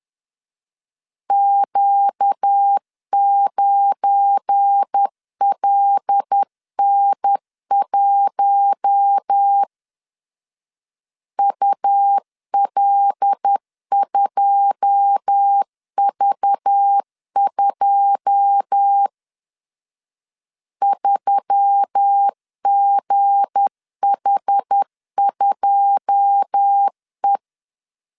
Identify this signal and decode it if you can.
Morse code, heard as a keyed tone, about 10 wpm, 800 Hz. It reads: Q9LN1 UL2V2 3GH2E
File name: LEZIONI / 9 2 10wpm 2736